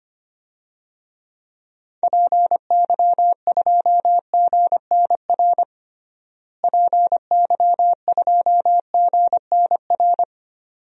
Morse code, heard as a keyed tone, about 25 wpm, 700 Hz. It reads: PY2GNR PY2GNR
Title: DXCC-100 Call Signs - 6 Characters - Repeated 2x - 25wpm